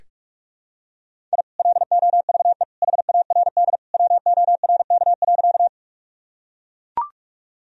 Morse code, heard as a keyed tone, about 42 wpm, 700 Hz. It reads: I LOVE HARD WORK.
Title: Sentences from Top 1000 Words - Review Part 1 - 45wpm